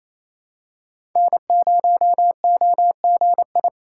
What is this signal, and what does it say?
Morse code, heard as a keyed tone, about 28 wpm, 700 Hz. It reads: N0OGI
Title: USA Call Signs - 5 Characters - 2x Character Spacing - 28wpm